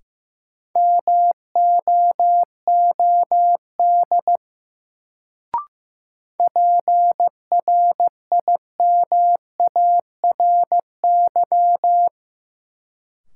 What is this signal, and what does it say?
Morse code, heard as a keyed tone, about 16 wpm, 700 Hz. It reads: MOOD PRIMARY
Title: Sentences from Top 1100 Words - New Word Drill Alt - 15wpm